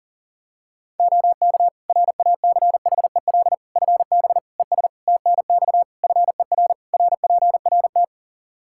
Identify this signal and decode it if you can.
Morse code, heard as a keyed tone, about 40 wpm, 700 Hz. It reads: OK RACHEL FB ES TNX FER RPRT
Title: Practice QSO - Exchange 2 - Info - 40wpm